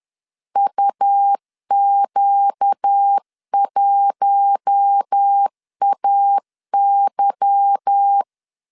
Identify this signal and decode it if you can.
Morse code, heard as a keyed tone, about 10 wpm, 800 Hz. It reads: UQ1AY